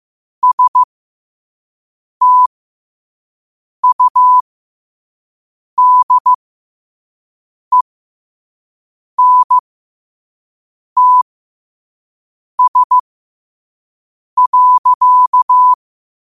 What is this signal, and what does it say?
Morse code, heard as a keyed tone, about 15 wpm, spaced as for 5 wpm, 1000 Hz. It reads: STUDENTS.
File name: qotdfiles / 5wpm